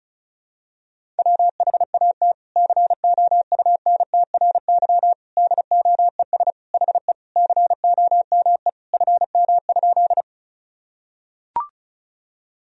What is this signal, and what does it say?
Morse code, heard as a keyed tone, about 34 wpm, 700 Hz. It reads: WHAT COUNTRY DOES HE COME FM?
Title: Sentences from Top 300 Words and Abbreviations 35wpm